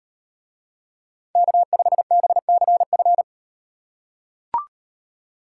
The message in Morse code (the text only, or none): K5BCF